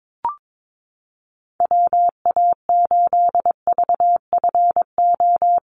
Morse code, heard as a keyed tone, about 22 wpm, 700 Hz. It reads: WA8VFO
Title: USA Call Signs Repeated 3x 22wpm